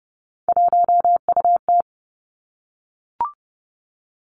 Morse code, heard as a keyed tone, about 32 wpm, 700 Hz. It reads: 1UT
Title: Three Challenging Letter-Number Combinations 30wpm